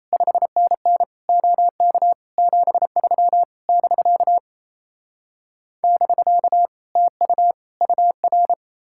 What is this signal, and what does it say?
Morse code, heard as a keyed tone, about 37 wpm, 700 Hz. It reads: 5NN OK 73 <BK> <BK> TU UR